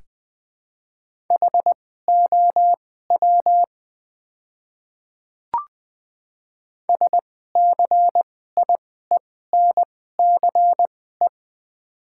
Morse code, heard as a keyed tone, about 21 wpm, spaced as for 15 wpm, 700 Hz. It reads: HOW SCIENCE
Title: 2x Character Spacing - Top 500 Words - 20wpm